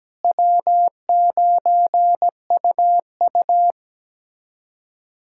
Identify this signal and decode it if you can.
Morse code, heard as a keyed tone, about 17 wpm, 700 Hz. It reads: W9UU
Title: USA Callsign DE Callsign - 4 Characters - 17wpm